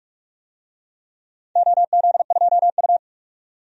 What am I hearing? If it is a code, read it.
Morse code, heard as a keyed tone, about 47 wpm, 700 Hz. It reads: OZ2U
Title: DXCC-100 Call Signs - 4 Characters - 45wpm